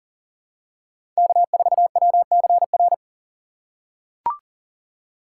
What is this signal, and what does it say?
Morse code, heard as a keyed tone, about 40 wpm, 690 Hz. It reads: K4WCR
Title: USA Call Signs - 3x Character Spacing - 40wpm